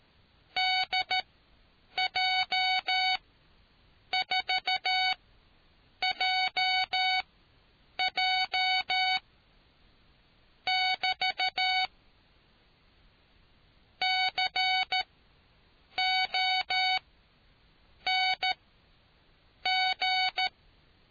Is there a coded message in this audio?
Yes: Morse code from 0.6 to 20.5 s, heard as a keyed tone, about 13 wpm, spaced as for 7 wpm, 750 Hz. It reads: DJ4JJ <BT> CONG